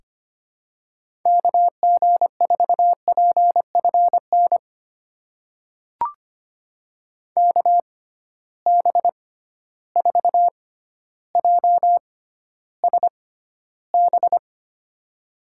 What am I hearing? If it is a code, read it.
Morse code, heard as a keyed tone, about 25 wpm, spaced as for 13 wpm, 700 Hz. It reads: KG4PFN KB4JSB
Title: USA Call Signs - 6 Characters - 6x Character Spacing - 25wpm